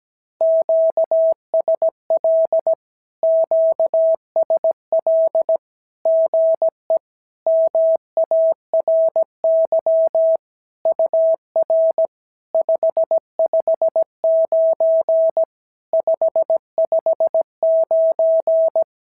Morse code, heard as a keyed tone, about 17 wpm, 650 Hz. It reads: QSL QSL GE MARY UR 559 559
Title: SOTA QSO Practice 17wpm